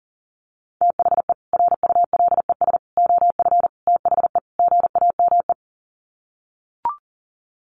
Morse code, heard as a keyed tone, about 40 wpm, 700 Hz. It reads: THE RULES OF THE GAME
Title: Sets of 5 Words Encore2 40wpm